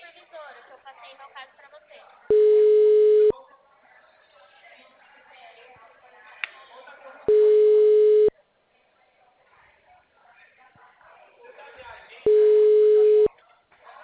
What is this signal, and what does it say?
Ringback tone (Europe, 425 Hz), with speech.
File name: out- -2003- - - .15152